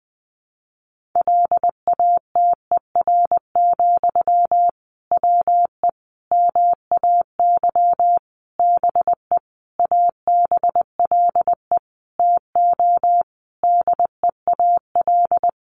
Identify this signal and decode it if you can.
Morse code, heard as a keyed tone, about 19 wpm, 700 Hz. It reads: LATER, WE MAY BE ABLE TO DEAL
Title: Sentences from Top 800 Words 20wpm